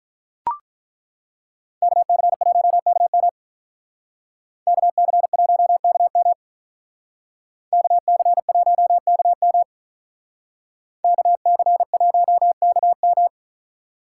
Morse code, heard as a keyed tone, about 42 wpm, 700 Hz. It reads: KC1KM KC1KM KC1KM KC1KM
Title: Speed Racing - USA 2x2 Call Signs 35wpm